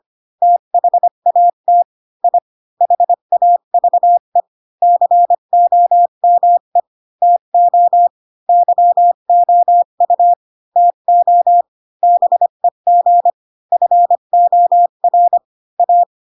Morse code, heard as a keyed tone, about 25 wpm, 700 Hz. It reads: THAT I HAVE COME TO YOU TO BEG FOR A